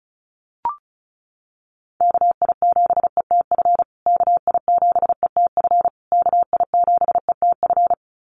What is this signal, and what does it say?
Morse code, heard as a keyed tone, about 34 wpm, 700 Hz. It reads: KI7ETF KI7ETF KI7ETF
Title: USA Call Signs Repeated 3x 35wpm